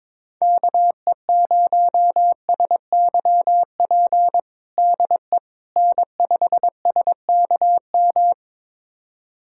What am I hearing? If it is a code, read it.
Morse code, heard as a keyed tone, about 22 wpm, 700 Hz. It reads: KE0SYP DE N5SKM
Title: USA Callsign DE Callsign 22wpm